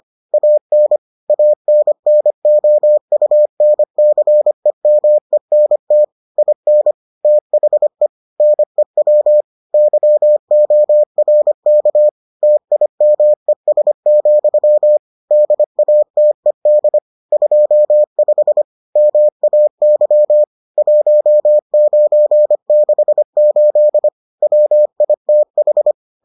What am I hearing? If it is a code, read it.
Morse code, heard as a keyed tone, about 25 wpm, 600 Hz. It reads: AN ANNOUNCEMENT IN THE NEW YORK TIMES, DATED 25 MAY 1968 WITH